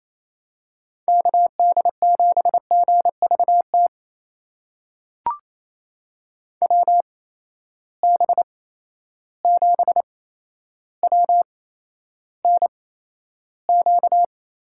Morse code, heard as a keyed tone, about 28 wpm, spaced as for 12 wpm, 700 Hz. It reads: KD7GVT WB7WNQ
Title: USA Call Signs - 6 Characters - 8x Character Spacing - 28wpm